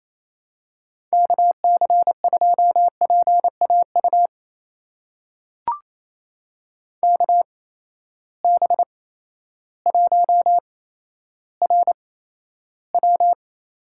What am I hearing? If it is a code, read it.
Morse code, heard as a keyed tone, about 28 wpm, spaced as for 13 wpm, 700 Hz. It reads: KC2PAU KB1RW
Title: USA Call Signs - 6 Characters - 8x Character Spacing - 28wpm